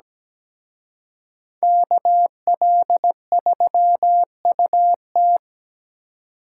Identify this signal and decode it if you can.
Morse code, heard as a keyed tone, about 20 wpm, 700 Hz. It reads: KL3UT
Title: Speed Racing - USA Call Signs 17wpm